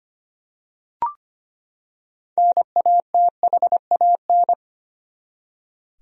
Morse code, heard as a keyed tone, about 25 wpm, 700 Hz. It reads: NATHAN
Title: USA Common Names - 25wpm